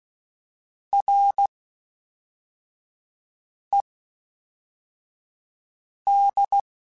Morse code, heard as a keyed tone, about 16 wpm, spaced as for 4 wpm, 800 Hz. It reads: RED
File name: Misc / Misc-mp3s / WordPairs sp @5